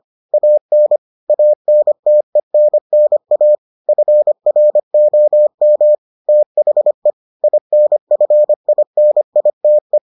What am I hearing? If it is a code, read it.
Morse code, heard as a keyed tone, about 25 wpm, 600 Hz. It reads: AN ANTENNA FROM THE INFINITE